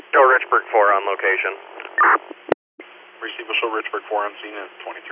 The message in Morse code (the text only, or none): none